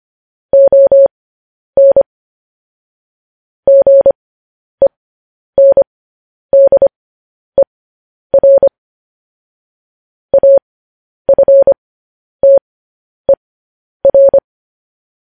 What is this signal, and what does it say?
Morse code, heard as a keyed tone, about 25 wpm, spaced as for 10 wpm, 550 Hz. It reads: ON GENDER AFTER
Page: news headlines in morse code at 10 wpm